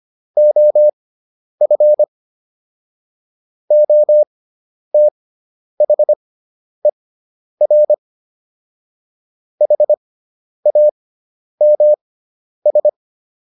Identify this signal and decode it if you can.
Morse code, heard as a keyed tone, about 25 wpm, spaced as for 10 wpm, 600 Hz. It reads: OF OTHER HAMS